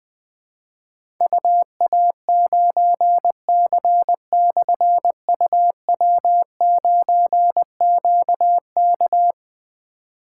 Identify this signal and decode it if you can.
Morse code, heard as a keyed tone, about 20 wpm, 700 Hz. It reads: UA9C/UW9QK
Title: Difficult Call Signs 20wpm